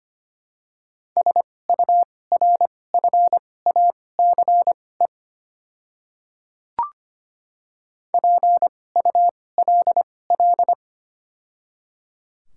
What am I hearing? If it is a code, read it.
Morse code, heard as a keyed tone, about 25 wpm, spaced as for 18 wpm, 700 Hz. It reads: SURFACE PULL
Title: 2x Character Spacing - Top 500 Words - 25wpm